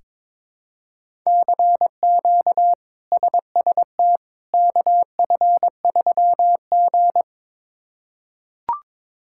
Morse code, heard as a keyed tone, about 22 wpm, 700 Hz. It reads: CQ SST KF3G
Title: K1USN SST QSO Practice - 22wpm